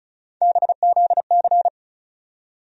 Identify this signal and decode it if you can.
Morse code, heard as a keyed tone, about 36 wpm, 700 Hz. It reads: BZC
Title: USA Call Sign Postfixes - 3 Letters 35wpm